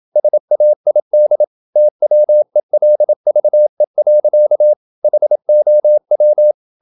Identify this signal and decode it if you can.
Morse code, heard as a keyed tone, about 27 wpm, 600 Hz. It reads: SAID TWELVE. HOW
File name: Never 0027wpm